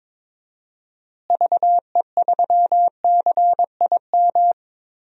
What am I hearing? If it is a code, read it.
Morse code, heard as a keyed tone, about 22 wpm, 700 Hz. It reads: VE3CIM
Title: DXCC-100 Call Signs - 6 Characters - Repeated 2x - 22wpm